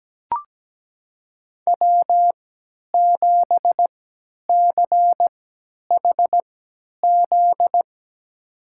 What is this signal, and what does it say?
Morse code, heard as a keyed tone, about 17 wpm, spaced as for 10 wpm, 700 Hz. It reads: W7CHZ
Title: USA Call Signs - 5 Characters - 3x Character Spacing - 17wpm